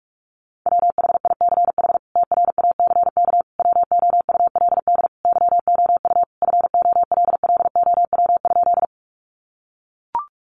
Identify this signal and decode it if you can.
Morse code, heard as a keyed tone, about 44 wpm, 700 Hz. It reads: WHICH TRACK WOULD YOU FOLLOW?
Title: Sentences from Top 1100 Words 45wpm